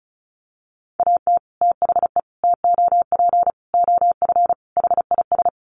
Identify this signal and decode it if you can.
Morse code, heard as a keyed tone, about 34 wpm, 700 Hz. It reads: AT THE TOP OF HIS
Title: Sets of 5 Words Encore 35wpm